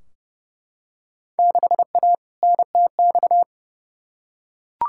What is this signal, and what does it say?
Morse code, heard as a keyed tone, about 30 wpm, 700 Hz. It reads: 6A NTX
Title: Field Day Exchanges - 30wpm